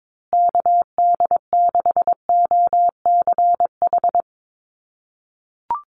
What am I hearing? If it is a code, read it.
Morse code, heard as a keyed tone, about 22 wpm, 700 Hz. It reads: KD6OCH
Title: USA Call Signs Repeated 3x 22wpm